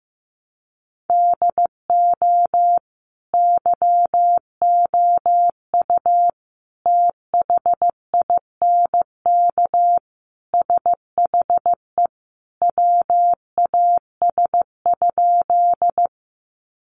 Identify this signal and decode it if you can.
Morse code, heard as a keyed tone, about 14 wpm, 700 Hz. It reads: DO YOU THINK SHE WAS?
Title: Sentences from Top 200 Words 15wpm